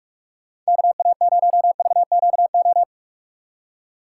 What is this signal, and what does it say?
Morse code, heard as a keyed tone, about 47 wpm, 700 Hz. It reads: KA0VQO
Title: USA Call Signs - 6 Characters - 3x Character Spacing - 45wpm